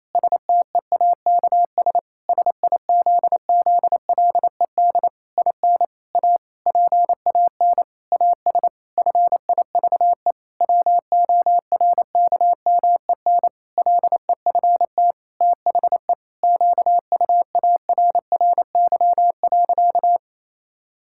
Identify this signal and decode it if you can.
Morse code, heard as a keyed tone, about 28 wpm, 700 Hz. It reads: STEAKS SIZZLED IN A PAN AS FIVE WORKMEN LEFT THE QUARRY.
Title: Pangram Challenge v2 - 28wpm